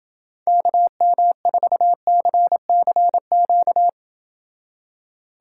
Morse code, heard as a keyed tone, about 28 wpm, 700 Hz. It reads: KM4CCQ